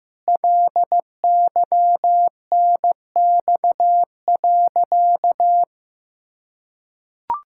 Morse code, heard as a keyed tone, about 15 wpm, 700 Hz. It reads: LYNX.
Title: Pangram Challenge v2 - 15wpm